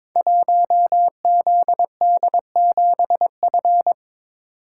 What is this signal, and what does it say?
Morse code, heard as a keyed tone, about 22 wpm, 700 Hz. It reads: 1ZD7F
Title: Code Groups - 5 Letter:Numbers - 22wpm